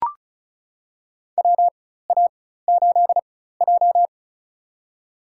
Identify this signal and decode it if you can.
Morse code, heard as a keyed tone, about 36 wpm, spaced as for 16 wpm, 700 Hz. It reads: WA8J
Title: USA Call Signs - 4x Character Spacing - 35wpm